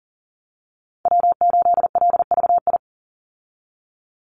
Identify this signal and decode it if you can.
Morse code, heard as a keyed tone, about 40 wpm, 700 Hz. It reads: W8LVI
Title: Call Signs Repeated 2x - 40wpm